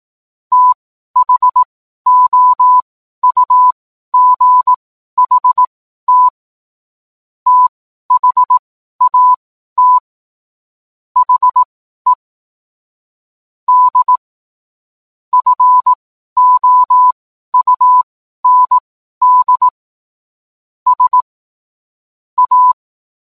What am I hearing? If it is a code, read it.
Morse code, heard as a keyed tone, about 18 wpm, spaced as for 12 wpm, 1000 Hz. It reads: THOUGHT THAT HE D FOUND S A